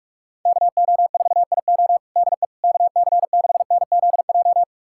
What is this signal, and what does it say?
Morse code, heard as a keyed tone, about 46 wpm, 700 Hz. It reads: KO4IO DE KC6NZJ
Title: USA Callsign DE Callsign 45wpm